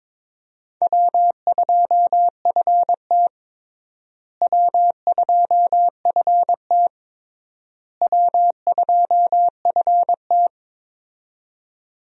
Morse code, heard as a keyed tone, about 22 wpm, 700 Hz. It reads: W2FT W2FT W2FT
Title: USA Call Signs - 4 Characters - Repeated 3x - 22wpm